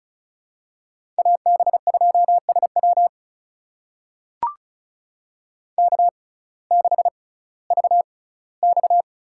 Morse code, heard as a keyed tone, about 36 wpm, spaced as for 19 wpm, 700 Hz. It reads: AB2SW K6VX
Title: USA Call Signs - 6x Character Spacing - 35wpm